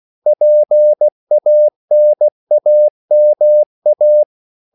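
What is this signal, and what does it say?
Morse code, heard as a keyed tone, about 16 wpm, 600 Hz. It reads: PANAMA
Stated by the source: CW practice of national capitals at 16 words per minute, from ON6ZQ.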